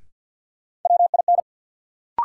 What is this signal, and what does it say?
Morse code, heard as a keyed tone, about 50 wpm, 700 Hz. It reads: WIN
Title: Sentences from Top 1500 Words - New Word Drill Alt - 50wpm